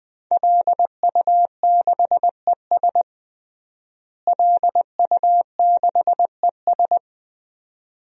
Morse code, heard as a keyed tone, about 20 wpm, 700 Hz. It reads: LU6ES LU6ES
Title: DXCC-100 Call Signs - 5 Characters - Repeated 2x - 20wpm